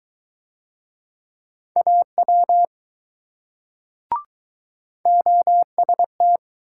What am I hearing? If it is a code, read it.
Morse code, heard as a keyed tone, about 20 wpm, 700 Hz. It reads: AW OST